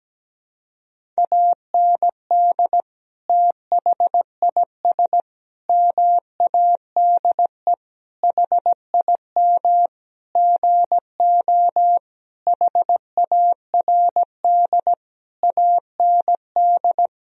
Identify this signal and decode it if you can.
Morse code, heard as a keyed tone, about 17 wpm, 700 Hz. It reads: AND THIS MADE HIM GO HARD AND
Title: Sentences from Top 1000 Words - Review Part 1 - 17wpm